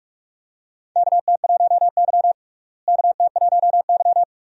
Morse code, heard as a keyed tone, about 46 wpm, 700 Hz. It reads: KT1Y KT1Y
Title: USA Call Signs - 4 Characters - Repeated 2x - 45wpm